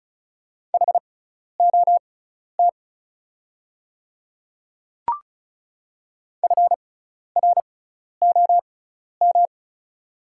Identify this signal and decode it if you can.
Morse code, heard as a keyed tone, about 36 wpm, spaced as for 12 wpm, 690 Hz. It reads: HOT FROM
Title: 6x Character Spacing - Top 100 Words - 35wpm